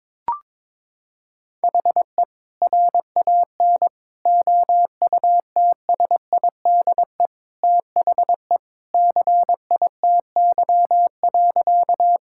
Morse code, heard as a keyed tone, about 22 wpm, 700 Hz. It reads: HE RAN OUTSIDE THE CITY.
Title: Sentences from Top 1000 Words - Review Part 6 - 22wpm